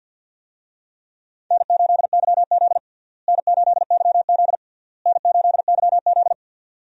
Morse code, heard as a keyed tone, about 48 wpm, 700 Hz. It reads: N8YZ N8YZ N8YZ
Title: USA Call Signs - 4 Characters - Repeated 3x - 50wpm